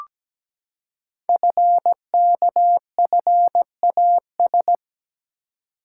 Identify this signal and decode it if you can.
Morse code, heard as a keyed tone, about 17 wpm, 700 Hz. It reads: FKFAS
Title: Code Groups - 5 Letter:Numbers - 17wpm